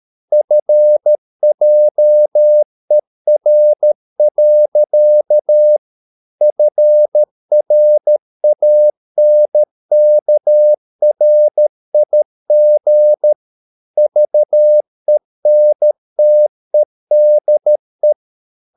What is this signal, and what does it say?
Morse code, heard as a keyed tone, about 13 wpm, 600 Hz. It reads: FJER. FRANKRIG VENTEDE
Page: Ferie 13 wpm